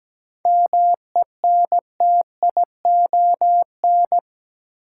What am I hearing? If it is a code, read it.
Morse code, heard as a keyed tone, about 17 wpm, 700 Hz. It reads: MENTION
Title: Seven Letter Words - 17wpm